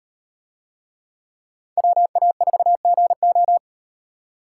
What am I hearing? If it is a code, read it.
Morse code, heard as a keyed tone, about 38 wpm, 700 Hz. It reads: WA4GO